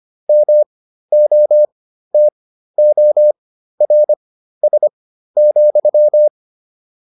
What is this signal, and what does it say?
Morse code, heard as a keyed tone, about 25 wpm, spaced as for 13 wpm, 600 Hz. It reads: MOTORS,